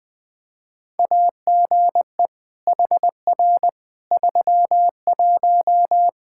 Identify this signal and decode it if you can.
Morse code, heard as a keyed tone, about 20 wpm, 700 Hz. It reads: AGE HR 31
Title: Practice QSO - Exchange 3 - More Info - 20wpm